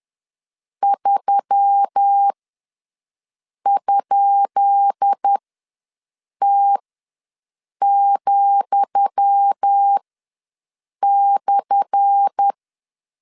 Morse code, heard as a keyed tone, about 10 wpm, spaced as for 6 wpm, 800 Hz. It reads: 3?T,/
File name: LEZIONI / 10 SIGNS / 10D COMA ws 10wpm 4314